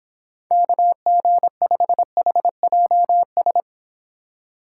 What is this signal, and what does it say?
Morse code, heard as a keyed tone, about 26 wpm, 700 Hz. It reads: KG5HJS